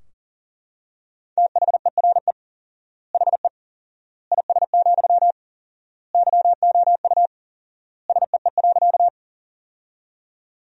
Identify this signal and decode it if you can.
Morse code, heard as a keyed tone, about 40 wpm, 700 Hz. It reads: THERE HE IS, YOU SEE.
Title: 4x Word-Spacing - Sentences from Top 100 Words - 40wpm